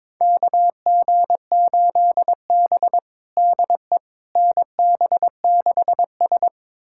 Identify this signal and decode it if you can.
Morse code, heard as a keyed tone, about 22 wpm, 700 Hz. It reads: KG8B DE NB6S
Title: USA Callsign DE Callsign - 4 Characters - 22wpm